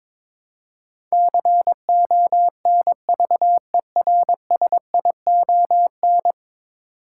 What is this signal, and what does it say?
Morse code, heard as a keyed tone, about 22 wpm, 700 Hz. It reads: CONVERSION
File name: practice / 22wpm / 10 Letter Words - First 3 Letters Given - 22wpm